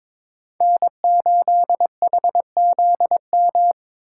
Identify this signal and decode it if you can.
Morse code, heard as a keyed tone, about 22 wpm, 700 Hz. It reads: N8HZM
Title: K1USN SST QSO Practice - 22wpm